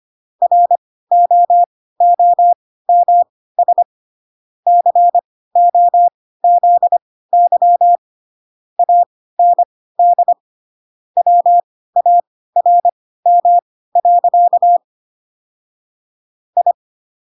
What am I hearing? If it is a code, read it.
Morse code, heard as a keyed tone, about 25 wpm, spaced as for 16 wpm, 700 Hz. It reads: ROOMS COZY AND WARM. I
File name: Pinocchio / Pinocchio ch1 10 25 ik1hge